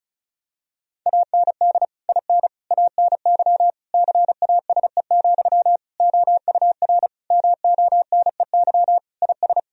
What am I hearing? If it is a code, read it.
Morse code, heard as a keyed tone, about 34 wpm, 700 Hz. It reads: AND IN ANY CASE, OUR MONEY IS
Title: Sentences from Top 1000 Words - Review Part 1 - 35wpm